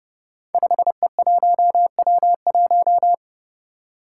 Morse code, heard as a keyed tone, about 28 wpm, 700 Hz. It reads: 5E1W1
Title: Five Challenging Letter-Number Combinations 30wpm